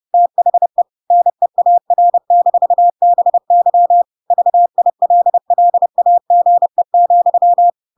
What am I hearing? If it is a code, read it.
Morse code, heard as a keyed tone, about 30 wpm, 700 Hz. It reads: THE NEAR-BY VILLAGE,
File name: Pinocchio / Pinocchio ch5 30 30 ik1hge